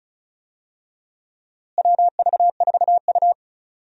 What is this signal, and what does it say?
Morse code, heard as a keyed tone, about 36 wpm, 700 Hz. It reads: WV4U
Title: USA Call Signs - 4 Characters - Repeated 2x - 35wpm